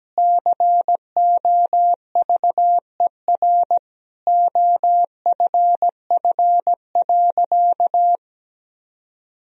Morse code, heard as a keyed tone, about 17 wpm, 700 Hz. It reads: COVER OFF.
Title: Sentences from Top 1000 Words - Review Part 1 - 17wpm